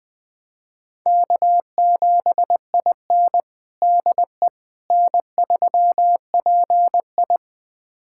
Morse code, heard as a keyed tone, about 21 wpm, 700 Hz. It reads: K7IN DE N3PI
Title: USA Callsign DE Callsign - 4 Characters - 20wpm